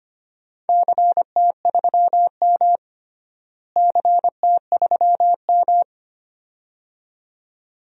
Morse code, heard as a keyed tone, about 25 wpm, 700 Hz. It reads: CT3M CT3M
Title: DXCC-100 Call Signs - 4 Characters - Repeated 2x - 25wpm